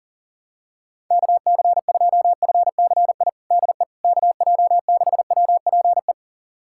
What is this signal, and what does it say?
Morse code, heard as a keyed tone, about 40 wpm, 700 Hz. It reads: KC2FCI DE KJ6WPE
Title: USA Callsign DE Callsign 40wpm